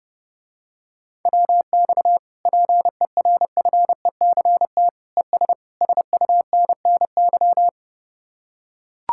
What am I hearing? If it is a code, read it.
Morse code, heard as a keyed tone, about 31 wpm, 700 Hz. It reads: WX PERFECT ES SUNNY